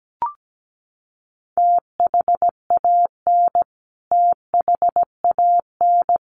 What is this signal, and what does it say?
Morse code, heard as a keyed tone, about 17 wpm, 700 Hz. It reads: THAN THAN
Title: Top 500 Words Repeated 17wpm